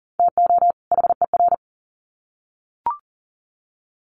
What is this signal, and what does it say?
Morse code, heard as a keyed tone, about 40 wpm, 700 Hz. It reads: TO HER
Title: Sets of 2 Words 40wpm